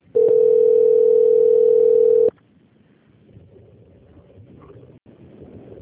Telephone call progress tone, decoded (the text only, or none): ringback tone